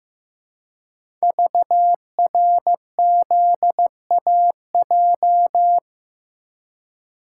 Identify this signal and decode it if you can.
Morse code, heard as a keyed tone, about 15 wpm, 700 Hz. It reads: VRZAJ
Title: Code Groups - 5 Letters - 15wpm